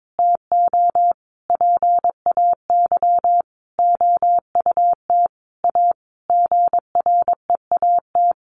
Morse code, heard as a keyed tone, about 22 wpm, 700 Hz. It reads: TO PAY OUT A GREAT
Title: Sentences from Top 1500 Words 22wpm